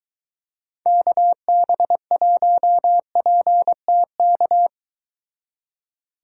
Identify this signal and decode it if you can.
Morse code, heard as a keyed tone, about 23 wpm, 700 Hz. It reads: KB1PTK